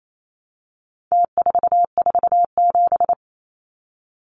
Morse code, heard as a keyed tone, about 28 wpm, 700 Hz. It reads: T447
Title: Numbers - Contest Style - 4-Digit 28wpm